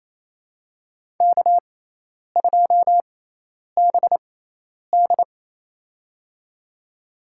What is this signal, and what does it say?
Morse code, heard as a keyed tone, about 28 wpm, spaced as for 10 wpm, 700 Hz. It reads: K2BD
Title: USA Call Signs - 6x Character Spacing - 28wpm